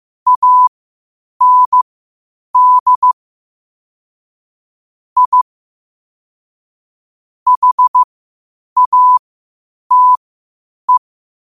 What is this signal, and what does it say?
Morse code, heard as a keyed tone, about 15 wpm, spaced as for 8 wpm, 1000 Hz. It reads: AND I HATE
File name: 8wpm